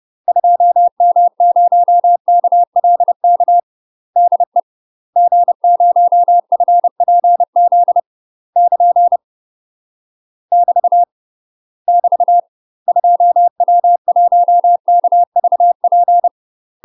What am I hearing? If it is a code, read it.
Morse code, heard as a keyed tone, about 30 wpm, 700 Hz. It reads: 2M0KLK DE G0FPZ <KN> <BT> <BT> 2W1KVP